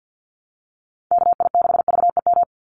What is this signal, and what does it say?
Morse code, heard as a keyed tone, about 50 wpm, 700 Hz. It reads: KI6VEM
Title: USA Call Signs 50wpm